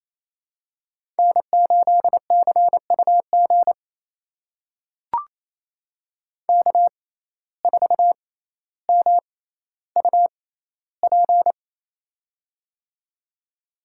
Morse code, heard as a keyed tone, about 28 wpm, spaced as for 14 wpm, 700 Hz. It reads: N8CUG K4MUP
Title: USA Call Signs - 5 Characters - 6x Character Spacing - 28wpm